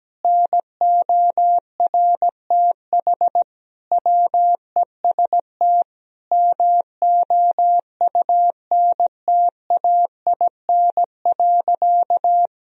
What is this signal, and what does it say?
Morse code, heard as a keyed tone, about 17 wpm, 700 Hz. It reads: NORTH WEST MOUNTAIN.
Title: Sentences from Top 1000 Words - Review Part 2 - 17wpm